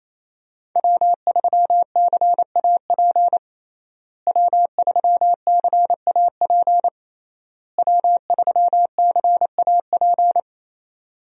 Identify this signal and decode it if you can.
Morse code, heard as a keyed tone, about 28 wpm, 700 Hz. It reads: W3CAP W3CAP W3CAP
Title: USA Call Signs - 5 Characters - Repeated 3x - 28wpm